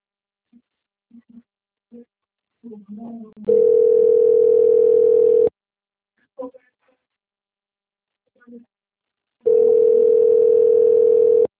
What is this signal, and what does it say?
Ringback tone (North America), 440 + 480 Hz, with speech.
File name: out- - - - - .94033